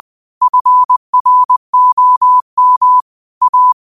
Morse code, heard as a keyed tone, about 20 wpm, 1000 Hz. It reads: FROM A
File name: qotdfiles / 20wpm